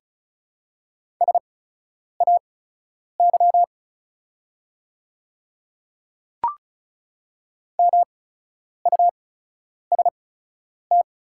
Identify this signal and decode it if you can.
Morse code, heard as a keyed tone, about 36 wpm, spaced as for 10 wpm, 690 Hz. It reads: SAY MUST
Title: 8x Character Spacing - Top 300 Words - 35wpm